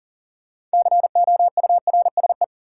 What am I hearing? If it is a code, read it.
Morse code, heard as a keyed tone, about 40 wpm, 700 Hz. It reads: COURSE
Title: Six Letter Words - 40wpm